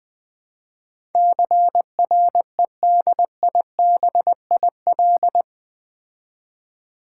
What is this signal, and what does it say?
Morse code, heard as a keyed tone, about 20 wpm, 700 Hz. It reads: CREDIBIL
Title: Build A Word - 4 Letters At A Time - 11-Letter Words - 20wpm